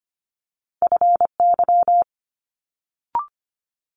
Morse code, heard as a keyed tone, about 25 wpm, 700 Hz. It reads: FY
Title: Multi-letter Phonemes Quick Fire 25wpm